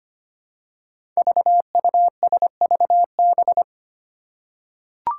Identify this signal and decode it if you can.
Morse code, heard as a keyed tone, about 25 wpm, 700 Hz. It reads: VUSVB
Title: Five Challenging Letter-Number Combinations 25wpm